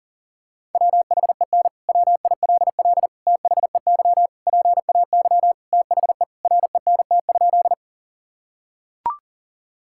Morse code, heard as a keyed tone, about 40 wpm, 700 Hz. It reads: WHEN WILL THEY PAY THE RENT?
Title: Sentences from Top 1000 Words - Review Part 5 - 40wpm